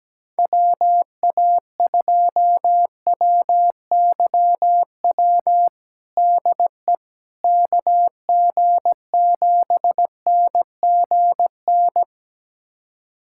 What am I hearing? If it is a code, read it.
Morse code, heard as a keyed tone, about 17 wpm, 700 Hz. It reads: WA2WYW DE KG7NGN
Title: USA Callsign DE Callsign - 6 Characters - 17wpm